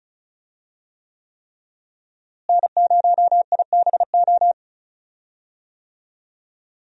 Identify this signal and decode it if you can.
Morse code, heard as a keyed tone, about 34 wpm, 700 Hz. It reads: N0IBO
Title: USA Call Signs - 5 Characters - Repeated 3x - 35wpm